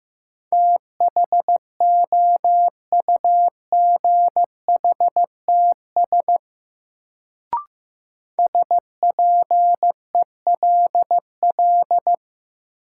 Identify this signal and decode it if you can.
Morse code, heard as a keyed tone, about 15 wpm, 700 Hz. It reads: THOUGHTS SPELL
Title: Sentences from Top 1000 Words - Review Part 1 - New Word Drill - 15wpm